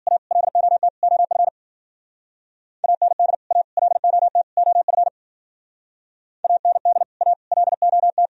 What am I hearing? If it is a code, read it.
Morse code, heard as a keyed tone, about 48 wpm, 700 Hz. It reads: A LOT OF AND A LOT OF AND A LOT
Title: Speed-Racing - Sets of 4 Words 40wpm